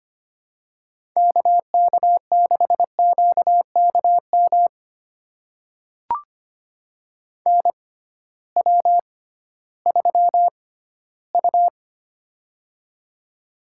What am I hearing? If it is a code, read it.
Morse code, heard as a keyed tone, about 25 wpm, spaced as for 15 wpm, 700 Hz. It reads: KK6QKM NW3U
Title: USA Call Signs - 6x Character Spacing - 25wpm